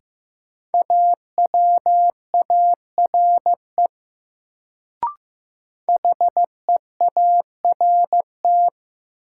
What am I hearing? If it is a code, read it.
Morse code, heard as a keyed tone, about 15 wpm, 700 Hz. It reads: AWARE HEART
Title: Sentences from Top 1200 Words - New Word Drill Alt - 15wpm